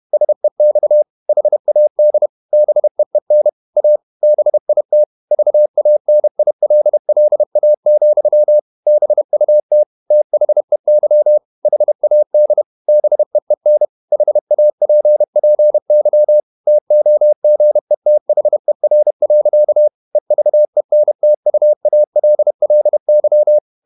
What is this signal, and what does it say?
Morse code, heard as a keyed tone, about 31 wpm, 600 Hz. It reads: SEX HAD BEEN A BIT VANILLA, BUT THEY HAD BEEN HAPPY TOGETHER. EVENTUALLY
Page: Never 31wpm | CW med Gnister